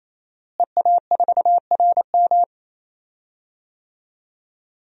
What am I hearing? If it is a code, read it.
Morse code, heard as a keyed tone, about 28 wpm, 700 Hz. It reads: EA4RM
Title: DXCC-100 Call Signs - 5 Characters - Repeated 3x - 28wpm